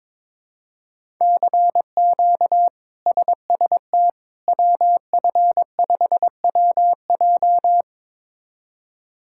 Morse code, heard as a keyed tone, about 22 wpm, 700 Hz. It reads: CQ SST WF5WJ
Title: K1USN SST QSO Practice - 22wpm